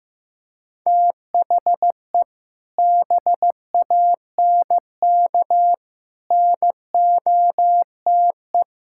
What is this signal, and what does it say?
Morse code, heard as a keyed tone, about 16 wpm, 700 Hz. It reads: THE BANK NOTE
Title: Sentences from Top 1000 Words - Review Part 6 - 15wpm